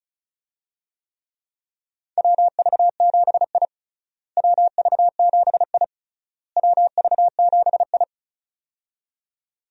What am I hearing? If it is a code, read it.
Morse code, heard as a keyed tone, about 34 wpm, 700 Hz. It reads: WV7I WV7I WV7I
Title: USA Call Signs - 4 Characters - Repeated 3x - 35wpm